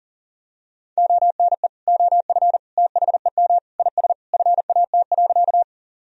Morse code, heard as a keyed tone, about 40 wpm, 700 Hz. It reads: ONE OF THEM IS FAT.
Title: Sentences from Top 1000 Words - Review Part 4 - 40wpm